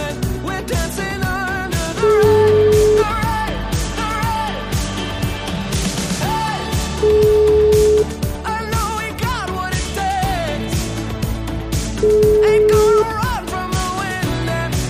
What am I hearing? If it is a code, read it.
Ringback tone (Europe, 425 Hz), with noise.